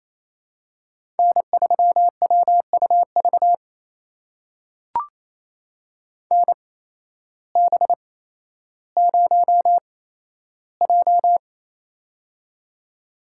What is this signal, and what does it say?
Morse code, heard as a keyed tone, about 28 wpm, spaced as for 13 wpm, 700 Hz. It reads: N3WUV NB0J
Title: USA Call Signs - 8x Character Spacing - 28wpm